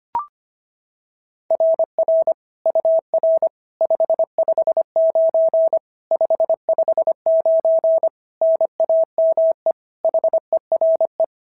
Morse code, heard as a keyed tone, about 24 wpm, 650 Hz. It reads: RR UR 559 559 NAME HERE
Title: SOTA QSO Practice 25wpm